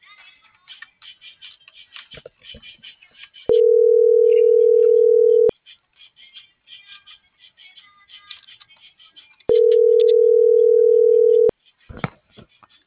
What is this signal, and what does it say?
Ringback tone (North America), 440 + 480 Hz.